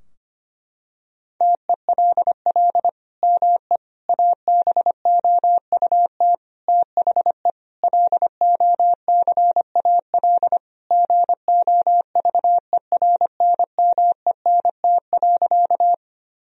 Morse code, heard as a keyed tone, about 25 wpm, 700 Hz. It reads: TELL ME ABOUT THE LOCAL GOVERNMENT.
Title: Sentences from Top 1000 Words - Review Part 3 - 25wpm